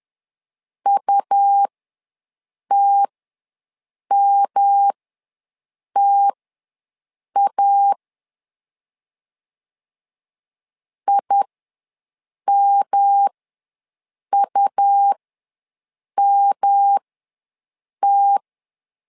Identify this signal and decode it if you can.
Morse code, heard as a keyed tone, about 10 wpm, spaced as for 6 wpm, 800 Hz. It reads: UTMTA IMUMT
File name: LEZIONI / 3 MWRO / 3 M ws 10wpm 2121